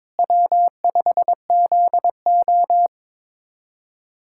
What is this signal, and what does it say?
Morse code, heard as a keyed tone, about 21 wpm, 700 Hz. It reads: W5ZO
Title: K1USN SST QSO Practice - 22wpm